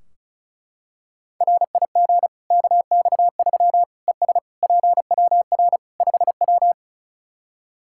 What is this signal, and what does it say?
Morse code, heard as a keyed tone, about 36 wpm, 700 Hz. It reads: RIG KX3 ES PWR 5W
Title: Practice QSO - Exchange 2 - Info - 35wpm